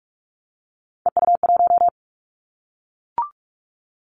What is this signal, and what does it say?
Morse code, heard as a keyed tone, about 44 wpm, 700 Hz. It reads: EU1
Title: Callsign Prefixes for Top 50 DXCC - 45wpm